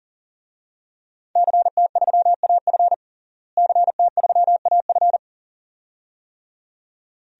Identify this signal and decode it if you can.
Morse code, heard as a keyed tone, about 40 wpm, 700 Hz. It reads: CT3AF CT3AF
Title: DXCC-100 Call Signs - 5 Characters - Repeated 2x - 40wpm